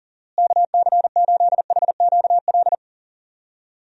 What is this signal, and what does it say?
Morse code, heard as a keyed tone, about 40 wpm, 700 Hz. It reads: KC8HQL
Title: USA Call Signs - 6 Characters - Repeated 2x - 40wpm